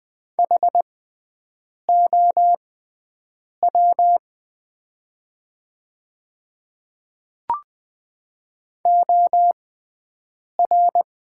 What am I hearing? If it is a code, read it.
Morse code, heard as a keyed tone, about 20 wpm, spaced as for 7 wpm, 700 Hz. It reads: HOW OR
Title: 6x Character Spacing - Top 200 Words - 20wpm